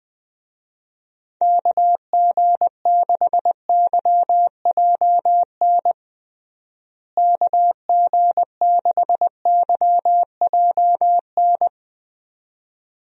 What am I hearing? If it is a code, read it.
Morse code, heard as a keyed tone, about 20 wpm, 700 Hz. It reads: KG6YJN KG6YJN
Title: USA Call Signs - 6 Characters - Repeated 2x - 20wpm